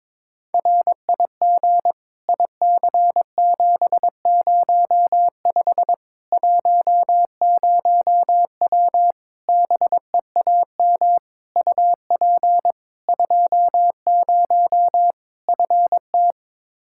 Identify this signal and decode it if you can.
Morse code, heard as a keyed tone, about 22 wpm, 700 Hz. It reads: RIG IC705 10W BEAM UP 20 FT
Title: Practice QSO - Exchange 2 - Info - 22wpm